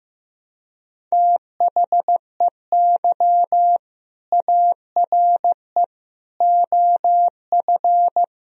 Morse code, heard as a keyed tone, about 16 wpm, 700 Hz. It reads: THEY ARE OF
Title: Sentences from Top 1000 Words - Review Part 4 - 15wpm